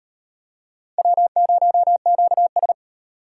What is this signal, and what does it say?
Morse code, heard as a keyed tone, about 36 wpm, 700 Hz. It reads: W0QS